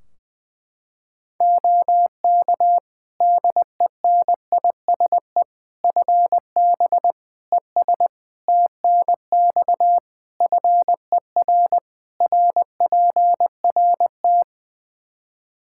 Morse code, heard as a keyed tone, about 20 wpm, 700 Hz. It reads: OK DENISE FB ES TNX FER RPRT
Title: K1USN SST QSO Practice 20wpm